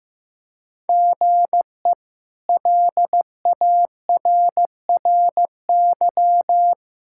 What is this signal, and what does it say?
Morse code, heard as a keyed tone, about 15 wpm, 700 Hz. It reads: GE LARRY
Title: K1USN SST QSO Practice - 15wpm